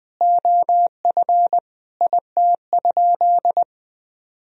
Morse code, heard as a keyed tone, about 21 wpm, 700 Hz. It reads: OF IT?
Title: Sentences from Top 100 Words and Abbreviations 20wpm